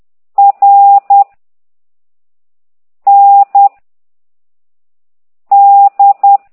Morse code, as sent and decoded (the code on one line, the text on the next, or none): .-. / -. / -..
R N D